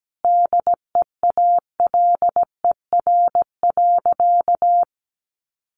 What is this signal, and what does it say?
Morse code, heard as a keyed tone, about 17 wpm, 700 Hz. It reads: DEALER.
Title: Sentences from Top 1100 Words 17wpm